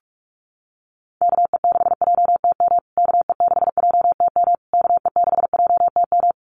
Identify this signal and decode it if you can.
Morse code, heard as a keyed tone, about 44 wpm, 700 Hz. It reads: KE6JTM KE6JTM KE6JTM
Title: USA Call Signs Repeated 3x 45wpm